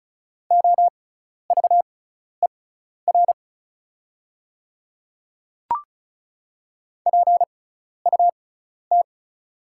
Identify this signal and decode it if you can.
Morse code, heard as a keyed tone, about 36 wpm, spaced as for 12 wpm, 700 Hz. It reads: OVER PUT
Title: 6x Character Spacing - Top 200 Words - 35wpm